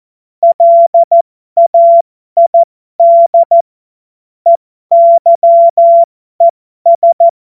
Morse code, heard as a keyed tone, about 14 wpm, spaced as for 12 wpm, 680 Hz. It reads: LAID EYES
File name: Morsekurs / Witze